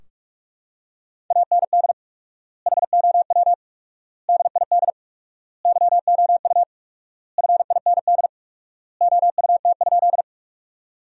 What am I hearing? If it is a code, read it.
Morse code, heard as a keyed tone, about 46 wpm, 700 Hz. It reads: AND HOW DID YOU FIND OUT?
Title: 4x Word-Spacing - Sentences from Top 200 Words - 45wpm